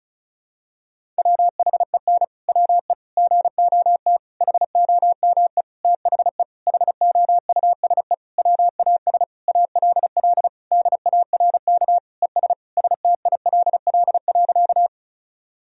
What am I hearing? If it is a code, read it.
Morse code, heard as a keyed tone, about 36 wpm, 700 Hz. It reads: WHEN WE GOT HOME THE HOUSE WAS ALL DARK ES STILL.
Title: Sentences from Top 500 Words and Abbreviations 35wpm